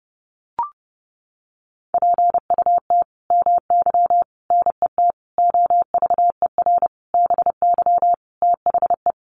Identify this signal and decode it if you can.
Morse code, heard as a keyed tone, about 30 wpm, 700 Hz. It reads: PUT MY NET OVER BY THE
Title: Sentences from Top 1400 Words 30wpm